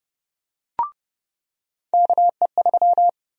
Morse code, heard as a keyed tone, about 30 wpm, 700 Hz. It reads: KE3
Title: USA Call Sign Prefixes - 30wpm